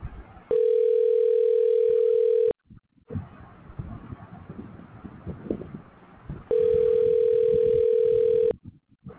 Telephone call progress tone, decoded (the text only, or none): ringback tone